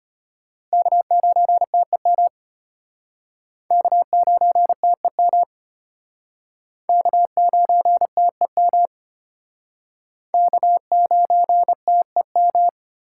Morse code, heard as a keyed tone, about 33 wpm, 700 Hz. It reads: K9TEM K9TEM K9TEM K9TEM